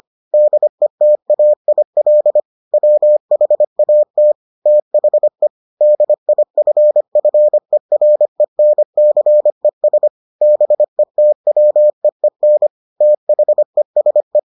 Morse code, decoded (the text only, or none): DETAIL WHAT THE DIFFERENCES BETWEEN THESE